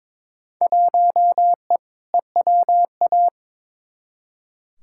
Morse code, heard as a keyed tone, about 22 wpm, 700 Hz. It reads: 1E EWA
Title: Field Day Exchanges - 22wpm